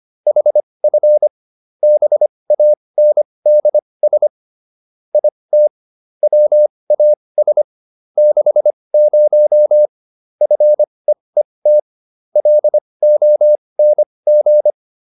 Morse code, heard as a keyed tone, about 25 wpm, spaced as for 20 wpm, 600 Hz. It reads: HF BANDS IT WAS 60 FEET LONG